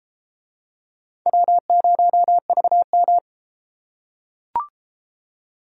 Morse code, heard as a keyed tone, about 31 wpm, 700 Hz. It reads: W0VM